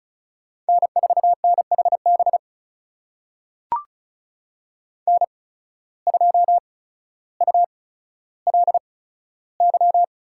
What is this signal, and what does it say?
Morse code, heard as a keyed tone, about 34 wpm, spaced as for 15 wpm, 700 Hz. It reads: N4NHB N2ULY
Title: USA Call Signs - 5 Characters - 8x Character Spacing - 35wpm